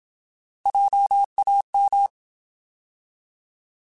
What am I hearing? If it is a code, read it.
Morse code, heard as a keyed tone, about 27 wpm, 800 Hz. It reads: JAM
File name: Alphabetical / Alphabetical mp3s / Js @25